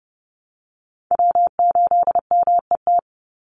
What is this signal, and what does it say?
Morse code, heard as a keyed tone, about 30 wpm, 700 Hz. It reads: W8MET
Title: Call Signs Repeated 2x - 30wpm